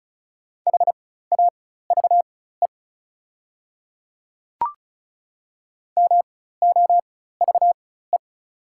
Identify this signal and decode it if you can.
Morse code, heard as a keyed tone, about 36 wpm, spaced as for 16 wpm, 700 Hz. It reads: HAVE MOVE
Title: 4x Character Spacing - Top 200 Words - 35wpm